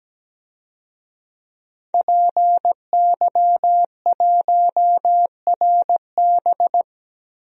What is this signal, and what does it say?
Morse code, heard as a keyed tone, about 17 wpm, 700 Hz. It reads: PY1RB
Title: DXCC-100 Call Signs - 5 Characters - Repeated 2x - 17wpm